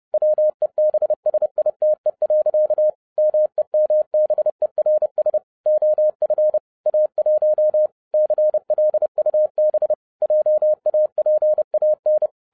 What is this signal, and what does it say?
Morse code, heard as a keyed tone, about 30 wpm, 610 Hz. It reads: WEBSITE. MEMBERS OF A1 CLUB JAPAN